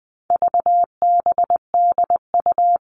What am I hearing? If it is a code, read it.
Morse code, heard as a keyed tone, about 20 wpm, 700 Hz. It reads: VBDU
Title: All Cut Numbers - 4-Digit 20wpm